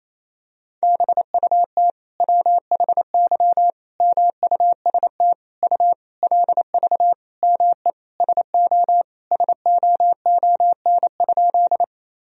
Morse code, heard as a keyed tone, about 28 wpm, 700 Hz. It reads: BUT WHY MUST U LV ME SO SOON?
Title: Sentences from Top 400 Words and Abbreviations 28wpm